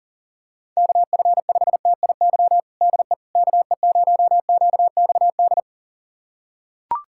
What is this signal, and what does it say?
Morse code, heard as a keyed tone, about 40 wpm, 700 Hz. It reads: KF5TIY DE KE0QXD
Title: USA Callsign DE Callsign - 6 Characters - 40wpm